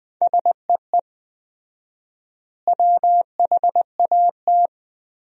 Morse code, heard as a keyed tone, about 20 wpm, 700 Hz. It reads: SEE WHAT
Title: 4x Word-Spacing - Sentences from Top 500 Words - 20wpm